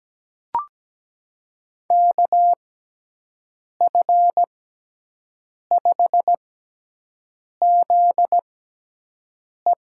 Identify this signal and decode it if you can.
Morse code, heard as a keyed tone, about 17 wpm, spaced as for 6 wpm, 700 Hz. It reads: KF5ZE
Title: USA Call Signs - 6 Characters - 6x Character Spacing - 17wpm